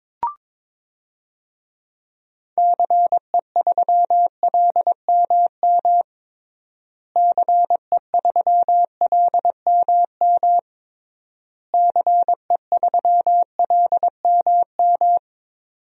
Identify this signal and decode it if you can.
Morse code, heard as a keyed tone, about 22 wpm, 700 Hz. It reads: CE3LMM CE3LMM CE3LMM
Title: DXCC-100 Call Signs - 6 Characters - Repeated 3x - 22wpm